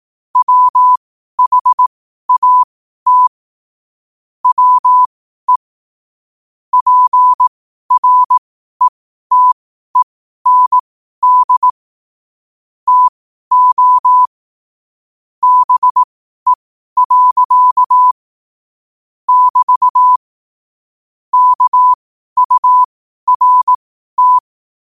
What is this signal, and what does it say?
Morse code, heard as a keyed tone, about 18 wpm, spaced as for 12 wpm, 1000 Hz. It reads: WHAT WE PRETEND TO BE. <BT> KURT
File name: qotdfiles / 12wpm